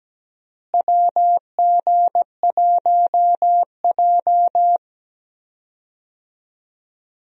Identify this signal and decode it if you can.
Morse code, heard as a keyed tone, about 17 wpm, 700 Hz. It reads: WG1J
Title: USA Call Signs - 4 Characters - Repeated 3x - 17wpm